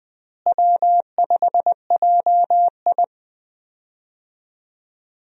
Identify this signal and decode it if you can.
Morse code, heard as a keyed tone, about 20 wpm, 700 Hz. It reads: W5JI
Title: USA Call Signs - 4 Characters - Repeated 3x - 20wpm